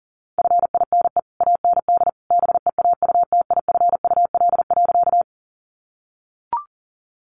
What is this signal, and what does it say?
Morse code, heard as a keyed tone, about 40 wpm, 700 Hz. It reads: FINE AND BEAUTIFUL.
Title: Sentences from Top 900 Words 40wpm